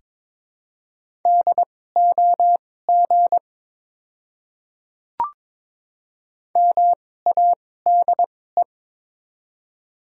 Morse code, heard as a keyed tone, about 22 wpm, spaced as for 16 wpm, 700 Hz. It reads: DOG MADE
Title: 2x Character Spacing - Top 500 Words - 22wpm